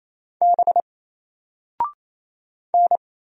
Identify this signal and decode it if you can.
Morse code, heard as a keyed tone, about 27 wpm, 700 Hz. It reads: B N
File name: Single-Letter-Number-28wpm